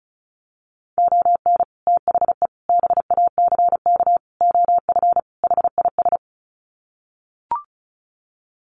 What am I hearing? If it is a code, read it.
Morse code, heard as a keyed tone, about 34 wpm, 700 Hz. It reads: ON THE BACK OF HIS
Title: Sets of 5 Words Encore 35wpm